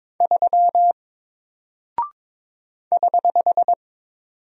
Morse code, heard as a keyed tone, about 21 wpm, 700 Hz. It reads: 3 <HH>
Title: Single Character (Letter:Number:Punctuation:Prosign) Alt 22wpm